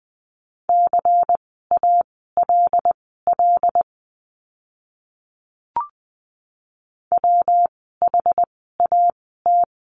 Morse code, heard as a keyed tone, about 20 wpm, spaced as for 14 wpm, 700 Hz. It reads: CALL WHAT
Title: 230 Lesson 37f - Letter x - Words 20wpm